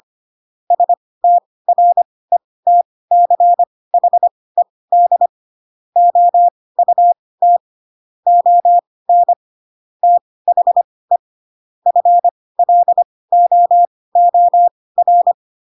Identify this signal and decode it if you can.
Morse code, heard as a keyed tone, about 25 wpm, spaced as for 18 wpm, 700 Hz. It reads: STRETCHED OUT ON THE FLOOR
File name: Pinocchio / Pinocchio ch7 12 25 ik1hge